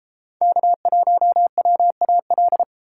Morse code, heard as a keyed tone, about 30 wpm, 700 Hz. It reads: K1WAL